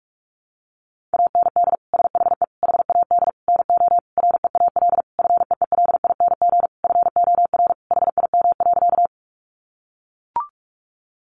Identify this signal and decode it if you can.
Morse code, heard as a keyed tone, about 44 wpm, 700 Hz. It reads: AND SHE HAD NO REAL FEELING FOR HIM.
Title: Sentences from Top 1200 Words 45wpm